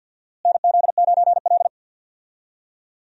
Morse code, heard as a keyed tone, about 51 wpm, 700 Hz. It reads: NZ9L
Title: USA Call Signs - 4 Characters - Repeated 3x - 50wpm